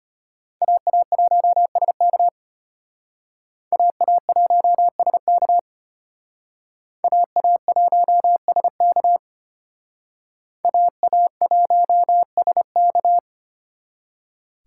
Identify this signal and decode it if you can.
Morse code, heard as a keyed tone, about 32 wpm, 700 Hz. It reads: AA1SK AA1SK AA1SK AA1SK